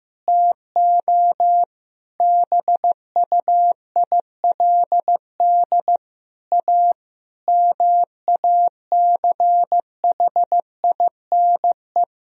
Morse code, heard as a keyed tone, about 15 wpm, 700 Hz. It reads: TO BUILD A MACHINE
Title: Sentences from Top 1000 Words - Review Part 2 - 15wpm